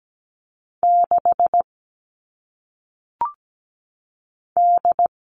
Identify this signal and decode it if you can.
Morse code, heard as a keyed tone, about 17 wpm, 700 Hz. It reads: 6 D
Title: All Cut Numbers - 1-Digit 17wpm